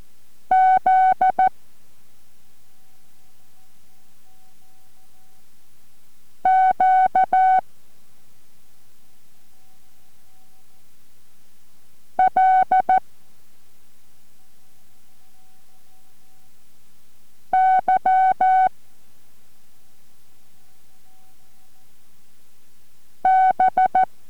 Morse code, heard as a keyed tone, about 14 wpm, spaced as for 2 wpm, 750 Hz. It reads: ZQLYB